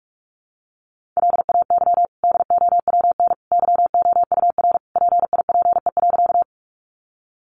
Morse code, heard as a keyed tone, about 46 wpm, 700 Hz. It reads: LAY DOWN YOUR PIPE.
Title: Sentences from Top 1500 Words 45wpm